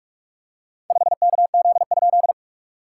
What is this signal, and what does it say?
Morse code, heard as a keyed tone, about 46 wpm, 690 Hz. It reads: 5KZ?
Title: SOTA QSO Practice 45wpm